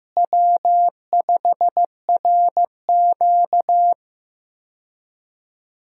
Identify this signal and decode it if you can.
Morse code, heard as a keyed tone, about 15 wpm, 700 Hz. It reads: W5RQ
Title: USA Call Signs - 4 Characters - Repeated 2x - 15wpm